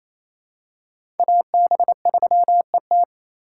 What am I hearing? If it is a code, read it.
Morse code, heard as a keyed tone, about 28 wpm, 700 Hz. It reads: AB3ET
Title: USA Call Signs - 4x Character Spacing - 28wpm